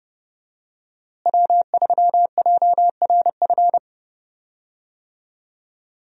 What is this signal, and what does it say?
Morse code, heard as a keyed tone, about 30 wpm, 700 Hz. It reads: W3JRF
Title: USA Call Signs - 5 Characters - Repeated 2x - 30wpm